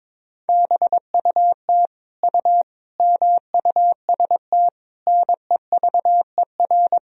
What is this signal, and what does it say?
Morse code, heard as a keyed tone, about 21 wpm, 700 Hz. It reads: BUT U MUST NEVER
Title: Sentences from Top 300 Words and Abbreviations 22wpm